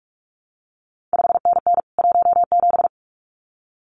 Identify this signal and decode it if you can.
Morse code, heard as a keyed tone, about 46 wpm, 700 Hz. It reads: 5NN 17
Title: CQ WW DX Contest Exchange Encore - 45wpm